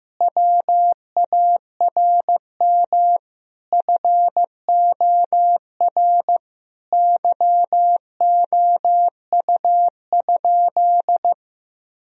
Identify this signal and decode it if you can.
Morse code, heard as a keyed tone, about 15 wpm, 700 Hz. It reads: WARM FOR YOU?
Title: Sentences from Top 1000 Words - Review Part 2 - 15wpm